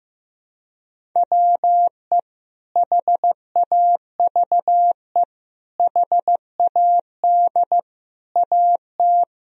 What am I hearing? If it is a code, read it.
Morse code, heard as a keyed tone, about 15 wpm, 700 Hz. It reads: WE HAVE HAD AT
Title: Sentences from Top 1000 Words - Review Part 1 - 15wpm